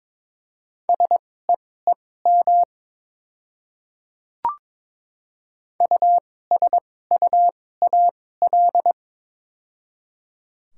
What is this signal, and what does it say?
Morse code, heard as a keyed tone, about 22 wpm, spaced as for 16 wpm, 700 Hz. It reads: SEEM USUAL
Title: 2x Character Spacing - Top 500 Words - 22wpm